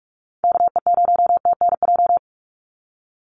Morse code, heard as a keyed tone, about 44 wpm, 700 Hz. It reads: KE0TNJ
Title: USA Call Signs 45wpm